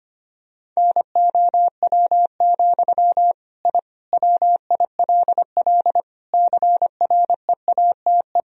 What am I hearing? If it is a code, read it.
Morse code, heard as a keyed tone, about 25 wpm, 700 Hz. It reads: NOW, I WILL CREATE
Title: Sentences from Top 1000 Words - Review Part 3 - 25wpm